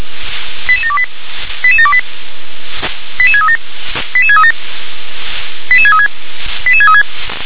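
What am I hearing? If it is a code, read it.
Five-tone selective call (ZVEI), heard as quick runs of five tones, tones of about 70 ms. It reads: 88728 88728 88637 88637 88636 88636
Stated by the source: Five-tone ZVEI selective call 88730 received on an SDR.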